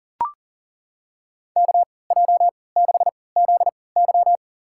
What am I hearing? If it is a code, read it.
Morse code, heard as a keyed tone, about 37 wpm, spaced as for 22 wpm, 700 Hz. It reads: KJ6ZY
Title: USA Call Signs - 3x Character Spacing - 40wpm